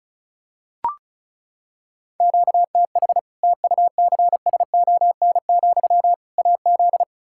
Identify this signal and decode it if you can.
Morse code, heard as a keyed tone, about 34 wpm, 700 Hz. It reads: QTH TUCSON, AZ
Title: Practice QSO - Exchange 1 - Beginning - 35wpm